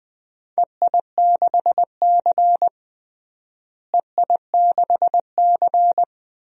Morse code, heard as a keyed tone, about 21 wpm, 700 Hz. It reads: EI6C EI6C
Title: DXCC-100 Call Signs - 4 Characters - Repeated 2x - 20wpm